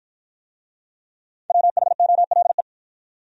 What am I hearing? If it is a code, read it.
Morse code, heard as a keyed tone, about 49 wpm, 700 Hz. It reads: WHOLE